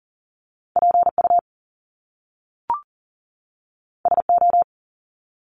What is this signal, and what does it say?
Morse code, heard as a keyed tone, about 40 wpm, 700 Hz. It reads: PU SO
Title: Top Two Letter Combinations in Words 40wpm